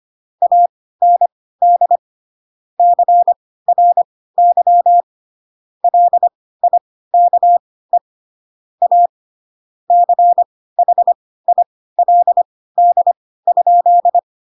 Morse code, heard as a keyed tone, about 25 wpm, spaced as for 16 wpm, 700 Hz. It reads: AND CRY LIKE A CHILD?
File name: Pinocchio / Pinocchio ch1 10 25 ik1hge